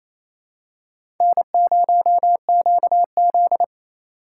Morse code, heard as a keyed tone, about 28 wpm, 700 Hz. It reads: N0QZ
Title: USA Call Signs - 4 Characters - 2x Character Spacing - 28wpm